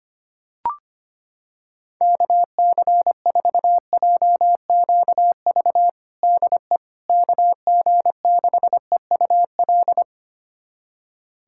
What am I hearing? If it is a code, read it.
Morse code, heard as a keyed tone, about 25 wpm, 700 Hz. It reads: KC4JQV DE KG6EUL
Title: USA Callsign DE Callsign - 6 Characters - 25wpm